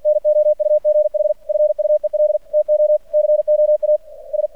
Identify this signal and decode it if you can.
Morse code, heard as a keyed tone, about 46 wpm, 600 Hz. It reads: NOUGH FUEL TO COAS
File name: archived / CRSnet